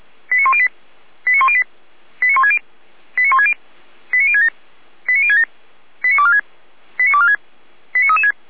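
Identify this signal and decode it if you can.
Five-tone selective call (ZVEI), heard as quick runs of five tones, tones of about 70 ms. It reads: 89108 89108 89160 89160 89067 89067 89246 89246 89336
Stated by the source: ZVEI five-tone selective call received on an SDR.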